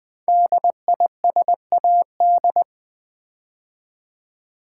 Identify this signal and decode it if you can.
Morse code, heard as a keyed tone, about 20 wpm, 700 Hz. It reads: DISAD
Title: Build A Word - 5 Letters At A Time - 12-Letter Words - 20wpm